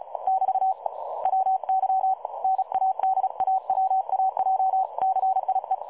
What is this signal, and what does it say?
Morse code, heard as a keyed tone, about 35 wpm, 760 Hz. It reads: <BT> WY NADAWAJCIE
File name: dlazaawansowanych / nagraniazpasma / websdr recording 3565.0kHz